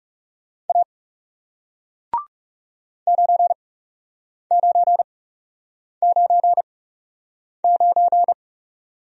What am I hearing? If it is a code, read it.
Morse code, heard as a keyed tone, about 40 wpm, spaced as for 8 wpm, 700 Hz. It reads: A 9999